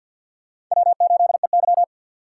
Morse code, heard as a keyed tone, about 48 wpm, 700 Hz. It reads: W8EY
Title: USA Call Signs - 4 Characters - 2x Character Spacing - 50wpm